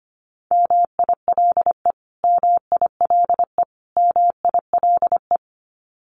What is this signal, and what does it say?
Morse code, heard as a keyed tone, about 25 wpm, 700 Hz. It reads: MILE MILE MILE
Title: Top 500 Words Repeated 25wpm